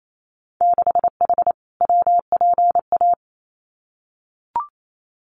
Morse code, heard as a keyed tone, about 28 wpm, 700 Hz. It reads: 6H WPA
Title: Winter Field Day Exchanges Encore 28wpm